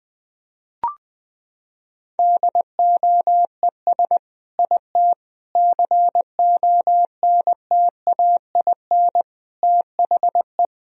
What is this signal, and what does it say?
Morse code, heard as a keyed tone, about 20 wpm, 700 Hz. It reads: DOES IT CONTAIN THE
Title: Sentences from Top 1000 Words - Review Part 2 - 20wpm